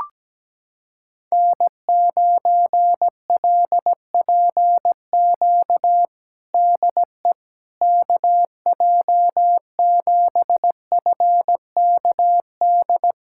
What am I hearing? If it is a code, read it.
Morse code, heard as a keyed tone, about 17 wpm, 700 Hz. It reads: N9LPQ DE KJ7FKD
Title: USA Callsign DE Callsign 17wpm